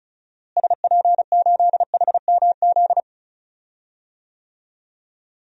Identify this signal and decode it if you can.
Morse code, heard as a keyed tone, about 34 wpm, 700 Hz. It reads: SP8HMZ
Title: DXCC-100 Call Signs - 6 Characters - Repeated 3x - 35wpm